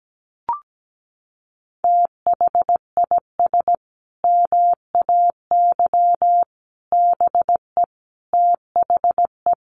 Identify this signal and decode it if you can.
Morse code, heard as a keyed tone, about 17 wpm, 700 Hz. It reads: THIS MAY BE THE
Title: Sentences from Top 900 Words 17wpm